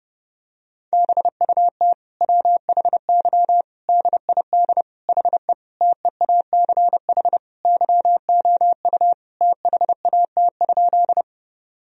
Morse code, heard as a keyed tone, about 30 wpm, 700 Hz. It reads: BUT WHY DID HE TEACH YOU THAT?
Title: Sentences from Top 1000 Words - Review Part 2 - 30wpm